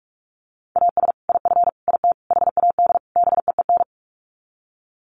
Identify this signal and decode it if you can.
Morse code, heard as a keyed tone, about 44 wpm, 700 Hz. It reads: AS IF IT HAD BEEN
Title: Sets of 5 Words Encore2 45wpm